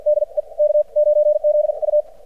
Morse code, heard as a keyed tone, about 50 wpm, spaced as for 40 wpm, 600 Hz. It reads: DE K9ZU
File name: archived / CRSnet